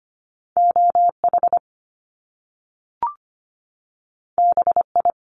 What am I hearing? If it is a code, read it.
Morse code, heard as a keyed tone, about 25 wpm, 700 Hz. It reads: OH BI
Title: Top Two Letter Combinations in Words 25wpm